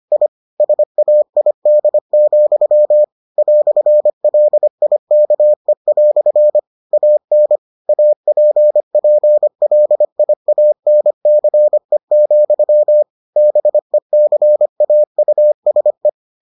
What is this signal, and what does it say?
Morse code, heard as a keyed tone, about 25 wpm, 600 Hz. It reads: I SAID, "LIKE" AN APPLIANCE, BECAUSE